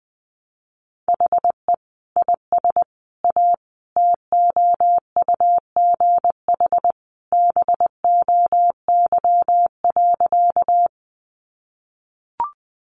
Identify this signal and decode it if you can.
Morse code, heard as a keyed tone, about 20 wpm, 700 Hz. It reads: HE IS A TOUGH BOY.
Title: Sentences from Top 1200 Words 20wpm